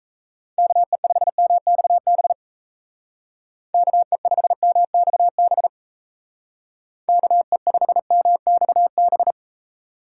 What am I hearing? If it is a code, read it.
Morse code, heard as a keyed tone, about 37 wpm, 700 Hz. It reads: KE5MXB KE5MXB KE5MXB